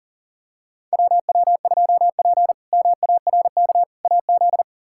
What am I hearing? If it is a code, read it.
Morse code, heard as a keyed tone, about 37 wpm, 700 Hz. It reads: WW2P MARK AZ
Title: K1USN SST QSO Practice - 40wpm